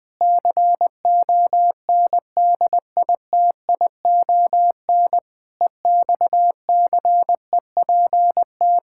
Morse code, heard as a keyed tone, about 21 wpm, 700 Hz. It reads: CONDITION EXCEPT
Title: Pangram Challenge v2 - 20wpm